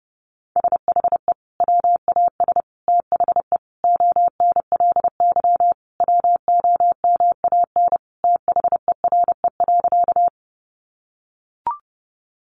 Morse code, heard as a keyed tone, about 30 wpm, 700 Hz. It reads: SHE WAS THE ONLY WOMAN THERE.
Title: Sentences from Top 1100 Words 30wpm